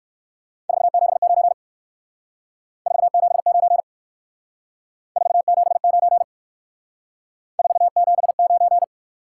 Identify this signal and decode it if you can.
Morse code, heard as a keyed tone, about 59 wpm, 700 Hz. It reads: 479 479 479 479